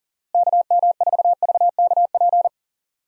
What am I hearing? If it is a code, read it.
Morse code, heard as a keyed tone, about 40 wpm, 700 Hz. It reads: KM4VKP
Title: USA Call Signs - 6 Characters - Repeated 2x - 40wpm